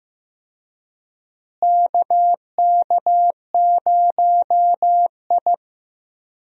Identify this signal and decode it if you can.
Morse code, heard as a keyed tone, about 15 wpm, 700 Hz. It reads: KK0I
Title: USA Call Signs - 4 Characters - Repeated 3x - 15wpm